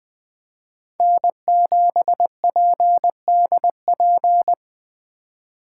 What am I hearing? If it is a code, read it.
Morse code, heard as a keyed tone, about 20 wpm, 700 Hz. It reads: N7PDP
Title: USA Call Signs - 5 Characters - 3x Character Spacing - 20wpm